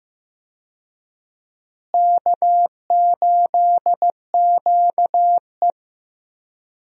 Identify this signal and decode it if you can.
Morse code, heard as a keyed tone, about 16 wpm, 700 Hz. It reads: K8QE
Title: USA Call Signs - 4 Characters - Repeated 2x - 15wpm